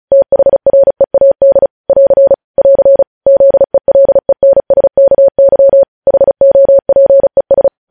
Morse code, heard as a keyed tone, about 36 wpm, 550 Hz. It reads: THREAD <AR> <AR> ZELENSKY HOPES
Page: News Headlines in Morse Code at 35 WPM